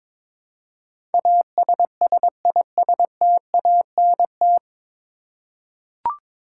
Morse code, heard as a keyed tone, about 23 wpm, 700 Hz. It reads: ASSISTANT
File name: 9 Letter Words - First 3 Letters Given - 22wpm